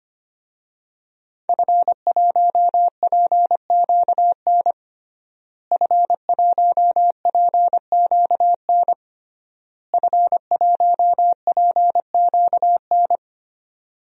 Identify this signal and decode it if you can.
Morse code, heard as a keyed tone, about 25 wpm, 700 Hz. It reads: F1PQN F1PQN F1PQN
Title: DXCC-100 Call Signs - 5 Characters - Repeated 3x - 25wpm